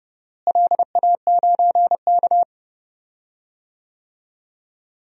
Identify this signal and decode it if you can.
Morse code, heard as a keyed tone, about 30 wpm, 700 Hz. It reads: LA9K
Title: DXCC-100 Call Signs - 4 Characters - Repeated 2x - 30wpm